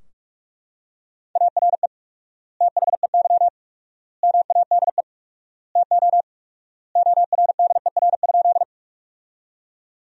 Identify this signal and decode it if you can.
Morse code, heard as a keyed tone, about 42 wpm, 700 Hz. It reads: ARE THEY MADE TO ORDER?
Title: 4x Word-Spacing - Sentences from Top 400 Words - 45wpm